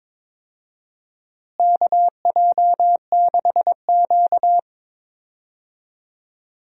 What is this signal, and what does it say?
Morse code, heard as a keyed tone, about 22 wpm, 700 Hz. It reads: KJ6Q
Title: USA Call Signs - 4 Characters - Repeated 3x - 22wpm